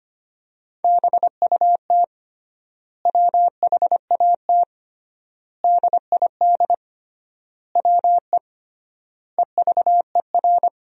Morse code, heard as a keyed tone, about 25 wpm, 700 Hz. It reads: BUT WHAT DID WE EVER
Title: 3x Word-Spacing - Sentences from Top 400 Words - 25wpm